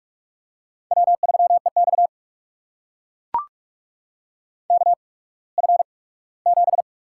Morse code, heard as a keyed tone, about 44 wpm, spaced as for 21 wpm, 700 Hz. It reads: W3EX KF7
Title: USA Call Signs - 8x Character Spacing - 45wpm